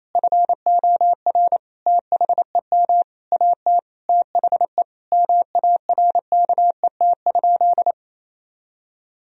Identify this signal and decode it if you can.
Morse code, heard as a keyed tone, about 28 wpm, 700 Hz. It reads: FOR THEM AT THE MARKET?
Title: Sentences from Top 1000 Words - Review Part 3 - 28wpm